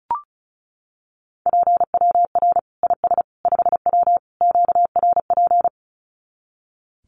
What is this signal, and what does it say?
Morse code, heard as a keyed tone, about 34 wpm, 700 Hz. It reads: PWR IS 5W QRP
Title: Common QSO Phrases 35wpm